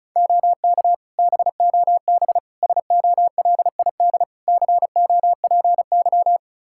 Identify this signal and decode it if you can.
Morse code, heard as a keyed tone, about 34 wpm, 700 Hz. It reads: OK BOB SOLID COPY
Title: Practice QSO - Exchange 3 - More Info - 35wpm